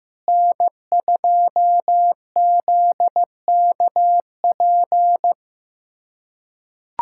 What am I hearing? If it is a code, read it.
Morse code, heard as a keyed tone, about 15 wpm, 700 Hz. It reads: N2ZKP
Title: USA Callsign DE Callsign - 5 Characters - 15wpm